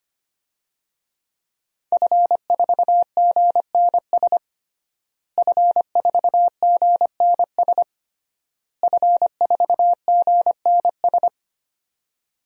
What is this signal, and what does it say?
Morse code, heard as a keyed tone, about 25 wpm, 700 Hz. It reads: F4GNS F4GNS F4GNS
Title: DXCC-100 Call Signs - 5 Characters - Repeated 3x - 25wpm